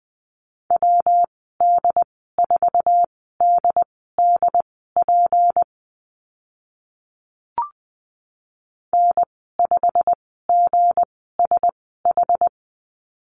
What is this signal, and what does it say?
Morse code, heard as a keyed tone, about 20 wpm, spaced as for 14 wpm, 700 Hz. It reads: WD4DDP N5GSH
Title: 161 Lesson 26g - Letter g - Call Signs 20wpm